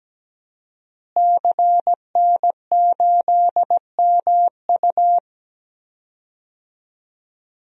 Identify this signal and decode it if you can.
Morse code, heard as a keyed tone, about 17 wpm, 700 Hz. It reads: CN8MU
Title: DXCC-100 Call Signs - 5 Characters - Repeated 3x - 17wpm